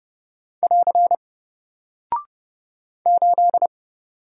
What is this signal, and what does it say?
Morse code, heard as a keyed tone, about 30 wpm, 700 Hz. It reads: <AR> 8
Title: Single Character (Letter:Number:Punctuation:Prosign) 30wpm